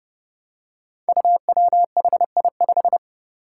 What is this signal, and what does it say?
Morse code, heard as a keyed tone, about 30 wpm, 700 Hz. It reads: UWHI5
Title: Five Challenging Letter-Number Combinations 30wpm